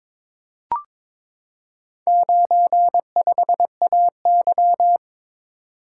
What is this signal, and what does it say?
Morse code, heard as a keyed tone, about 22 wpm, 700 Hz. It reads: 95AY
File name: practice / 22wpm / Four Letter-Number Combinations - 22wpm